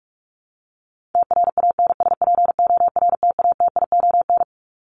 Morse code, heard as a keyed tone, about 44 wpm, 700 Hz. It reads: TRANSPORTATION
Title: Top 1501-1750 Words 45wpm